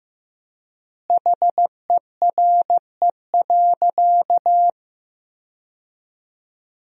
Morse code, heard as a keyed tone, about 15 wpm, 700 Hz. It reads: HERE.
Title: 4x Word-Spacing - Sentences from Top 200 Words - 15wpm